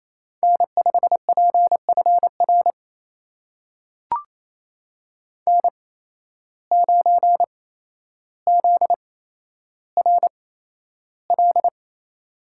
Morse code, heard as a keyed tone, about 28 wpm, spaced as for 12 wpm, 700 Hz. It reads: N5PFR N9ZRL
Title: USA Call Signs - 5 Characters - 8x Character Spacing - 28wpm